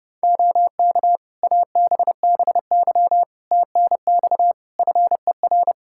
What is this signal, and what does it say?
Morse code, heard as a keyed tone, about 32 wpm, 700 Hz. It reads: OK ABBY TNX FER
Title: Practice QSO - Exchange 4 - Ending - 30wpm